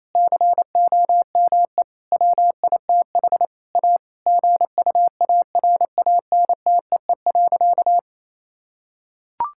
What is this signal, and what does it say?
Morse code, heard as a keyed tone, about 28 wpm, 700 Hz. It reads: COME WITH A GUARANTEE.
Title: Sentences from Top 1600 Words 28wpm